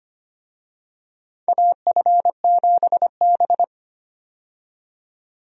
Morse code, heard as a keyed tone, about 25 wpm, 700 Hz. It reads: AF7B
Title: USA Call Signs - 4 Characters - 25wpm